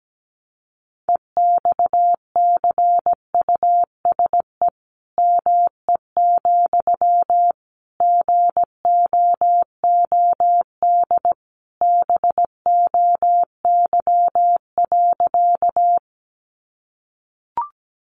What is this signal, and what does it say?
Morse code, heard as a keyed tone, about 17 wpm, 700 Hz. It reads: EXCUSE ME, GOOD BOY.
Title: Sentences from Top 700 Words 17wpm